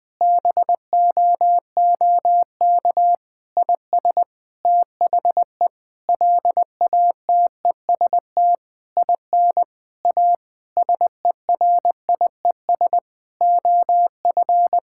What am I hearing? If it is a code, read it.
Morse code, heard as a keyed tone, about 21 wpm, 700 Hz. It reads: BOOK IS THE LATEST IN A SERIES OF
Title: Sentences with 7-Word Phrases - 20wpm